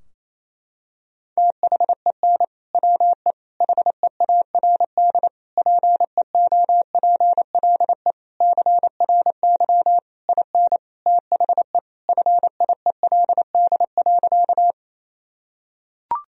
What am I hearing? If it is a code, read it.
Morse code, heard as a keyed tone, about 28 wpm, 700 Hz. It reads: THEN WE HEARD PEOPLE CRY IN THE FIELD.
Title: Sentences from Top 1000 Words - Review Part 2 - 28wpm